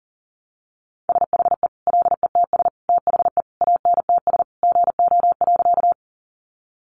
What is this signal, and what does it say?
Morse code, heard as a keyed tone, about 40 wpm, 700 Hz. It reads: SHE LETS THE ANTS GO.
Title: Easy Phrases 40wpm